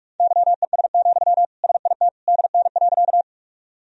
Morse code, heard as a keyed tone, about 46 wpm, 700 Hz. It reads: YES, SIT DN.
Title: Sentences from Top 500 Words and Abbreviations 45wpm